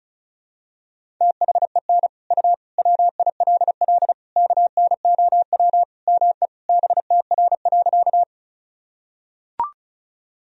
Morse code, heard as a keyed tone, about 36 wpm, 700 Hz. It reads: THEN U WILL KNOW ME BTR.
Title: Sentences from Top 400 Words and Abbreviations 35wpm